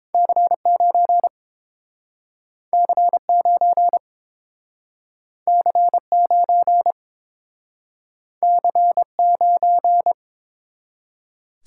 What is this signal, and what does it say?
Morse code, heard as a keyed tone, about 31 wpm, 700 Hz. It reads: C9 C9 C9 C9